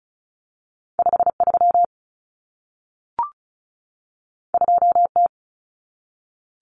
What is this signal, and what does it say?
Morse code, heard as a keyed tone, about 36 wpm, 700 Hz. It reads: 53 2T
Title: Common Cut Numbers - 2-Digit 35wpm